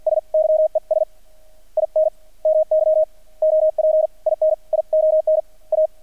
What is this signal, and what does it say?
Morse code, heard as a keyed tone, about 50 wpm, spaced as for 41 wpm, 630 Hz. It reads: S YES IN MY OPINION A
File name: archived / CRSnet